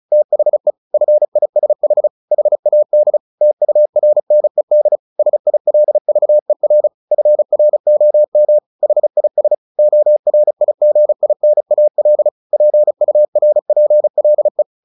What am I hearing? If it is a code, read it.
Morse code, heard as a keyed tone, about 34 wpm, 600 Hz. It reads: THE FISH HAD TURNED SILVER FROM HIS ORIGINAL PURPLE
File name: the old man and the sea 2 35wpm